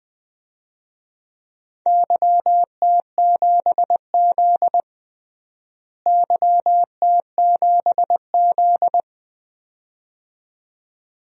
Morse code, heard as a keyed tone, about 20 wpm, 700 Hz. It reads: YT7Z YT7Z
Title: DXCC-100 Call Signs - 4 Characters - Repeated 2x - 20wpm